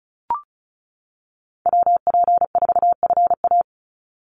Morse code, heard as a keyed tone, about 34 wpm, 700 Hz. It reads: WP4FA
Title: USA Call Signs 35wpm